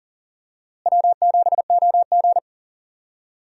Morse code, heard as a keyed tone, about 39 wpm, 700 Hz. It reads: W7OG
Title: USA Call Signs - 4 Characters - Repeated 3x - 40wpm